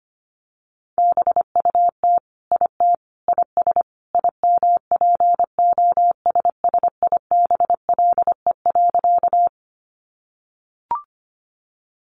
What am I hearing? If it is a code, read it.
Morse code, heard as a keyed tone, about 25 wpm, 700 Hz. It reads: BUT IT IS IMPOSSIBLE.
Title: Sentences from Top 1100 Words 25wpm